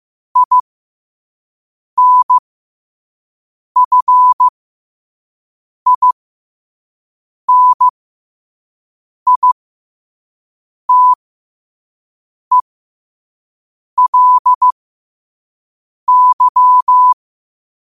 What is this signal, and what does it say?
Morse code, heard as a keyed tone, about 15 wpm, spaced as for 5 wpm, 1000 Hz. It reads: INFINITELY
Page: New quotes every day in morse code at 5 Words per minute.